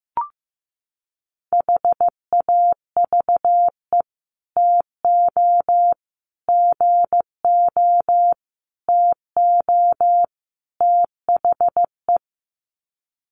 Morse code, heard as a keyed tone, about 15 wpm, 700 Hz. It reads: HAVE TO GO TO THE
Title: Sets of 5 Words Encore 15wpm